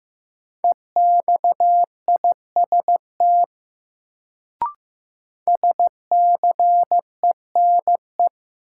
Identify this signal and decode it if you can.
Morse code, heard as a keyed tone, about 15 wpm, 700 Hz. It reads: EXIST SCENE
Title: Sentences from Top 1100 Words - New Word Drill Alt - 15wpm